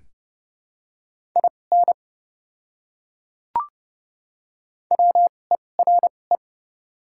Morse code, heard as a keyed tone, about 30 wpm, spaced as for 22 wpm, 700 Hz. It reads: IN WERE
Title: 2x Character Spacing - Top 100 Words - 30wpm